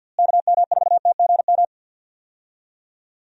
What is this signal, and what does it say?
Morse code, heard as a keyed tone, about 52 wpm, 700 Hz. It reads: KM4TGM
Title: USA Call Signs - 6 Characters - Repeated 2x - 50wpm